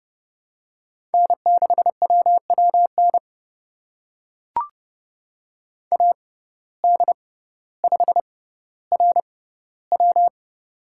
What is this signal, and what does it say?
Morse code, heard as a keyed tone, about 30 wpm, spaced as for 15 wpm, 700 Hz. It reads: N6WWN AD5RW
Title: USA Call Signs - 5 Characters - 6x Character Spacing - 30wpm